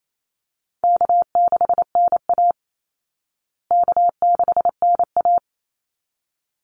Morse code, heard as a keyed tone, about 28 wpm, 700 Hz. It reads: K6NA K6NA
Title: Easy Call Signs Repeated 2x - 28wpm